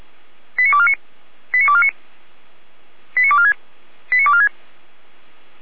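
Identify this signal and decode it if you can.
Five-tone selective call (ZVEI), heard as quick runs of five tones, tones of about 70 ms. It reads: 89249 89249 89246 89246
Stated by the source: ZVEI five-tone call 89235 received by SDR, archived by Feuerwehr Orlamünde fire brigade.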